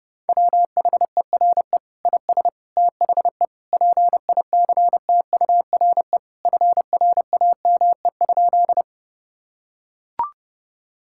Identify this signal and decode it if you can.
Morse code, heard as a keyed tone, about 30 wpm, 700 Hz. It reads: WHERE IS THE PICTURE FRAME?
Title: Sentences from Top 1000 Words - Review Part 6 - 30wpm